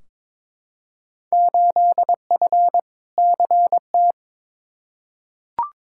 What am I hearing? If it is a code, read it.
Morse code, heard as a keyed tone, about 22 wpm, 700 Hz. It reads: 8F CT
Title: Field Day Exchanges Encore - 22wpm